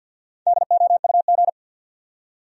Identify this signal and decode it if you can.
Morse code, heard as a keyed tone, about 53 wpm, 700 Hz. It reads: DOUG
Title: USA Common Names - 50wpm